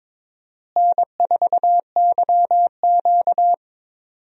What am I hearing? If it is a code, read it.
Morse code, heard as a keyed tone, about 22 wpm, 700 Hz. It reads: N4YQ
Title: USA Call Signs - 4 Characters - Repeated 2x - 22wpm